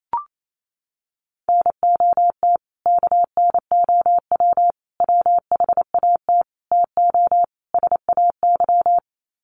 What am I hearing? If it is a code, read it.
Morse code, heard as a keyed tone, about 28 wpm, 700 Hz. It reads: NOT KNOW WHAT TO SAY
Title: Sets of 5 Words 28wpm